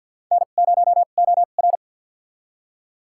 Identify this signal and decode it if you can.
Morse code, heard as a keyed tone, about 52 wpm, spaced as for 36 wpm, 700 Hz. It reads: N0OR
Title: USA Call Signs - 4 Characters - 2x Character Spacing - 50wpm